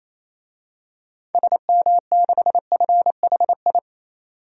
Morse code, heard as a keyed tone, about 28 wpm, 700 Hz. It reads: SM6FHI
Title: DXCC-100 Call Signs - 6 Characters - 28wpm